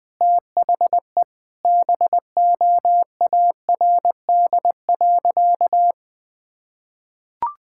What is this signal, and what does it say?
Morse code, heard as a keyed tone, about 20 wpm, 700 Hz. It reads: THE BOARD.
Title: Sentences with 3-Word Phrases - 20wpm